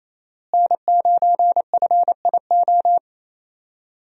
Morse code, heard as a keyed tone, about 28 wpm, 700 Hz. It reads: N9FIO
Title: USA Call Signs - 5 Characters - Repeated 3x - 28wpm